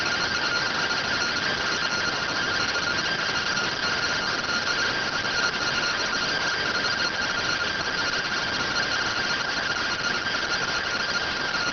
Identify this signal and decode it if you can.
Two-tone teleprinter signal, about 170 Hz shift.